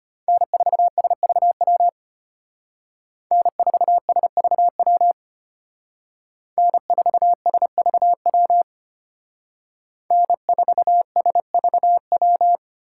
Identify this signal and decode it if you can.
Morse code, heard as a keyed tone, about 30 wpm, 700 Hz. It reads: N4SVW N4SVW N4SVW N4SVW
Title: Speed Racing - USA 1x3 Call Signs 25wpm